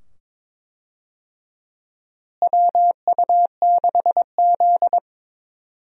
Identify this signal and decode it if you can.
Morse code, heard as a keyed tone, about 22 wpm, 700 Hz. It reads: WU6Z
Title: USA Call Signs - 4 Characters - Repeated 2x - 22wpm